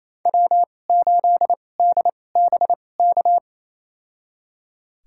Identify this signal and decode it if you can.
Morse code, heard as a keyed tone, about 28 wpm, spaced as for 20 wpm, 700 Hz. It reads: W8DBK
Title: USA Call Signs - 5 Characters - 2x Character Spacing - 28wpm